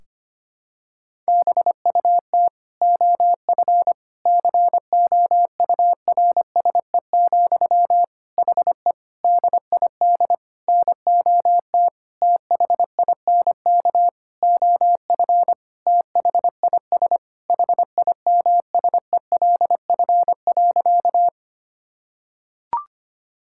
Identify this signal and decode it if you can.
Morse code, heard as a keyed tone, about 25 wpm, 700 Hz. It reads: BUT OF COURSE, HE DID NOT THINK OF THIS HIMSELF.
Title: Sentences from Top 1600 Words 25wpm